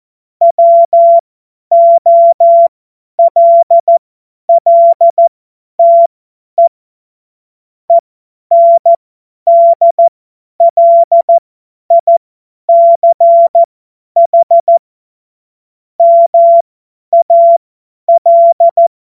Morse code, heard as a keyed tone, about 14 wpm, spaced as for 10 wpm, 680 Hz. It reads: WOLLTE ENDLICH MAL
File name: Morsekurs / Witze / Witz 08. August 2023 50 F70